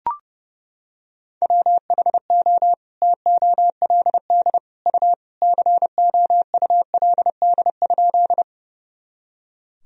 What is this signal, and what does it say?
Morse code, heard as a keyed tone, about 28 wpm, 700 Hz. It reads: WHO TOLD U COULD?
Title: Sentences from Top 400 Words and Abbreviations 30wpm